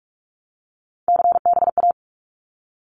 Morse code, heard as a keyed tone, about 44 wpm, 700 Hz. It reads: CBA
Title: Abbreviations Repeated 3x - 45wpm